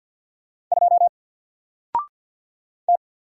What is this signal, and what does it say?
Morse code, heard as a keyed tone, about 49 wpm, 700 Hz. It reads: 2 T
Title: Single Character (Letter:Number:Punctuation:Prosign) Alt 50wpm